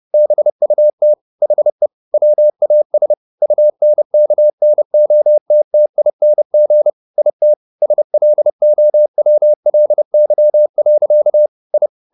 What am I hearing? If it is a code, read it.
Morse code, heard as a keyed tone, about 30 wpm, 600 Hz. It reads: BUT HE WAS UNKNOTTING IT SLOWLY. I
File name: the old man and the sea / the old man and the sea 1 30wpm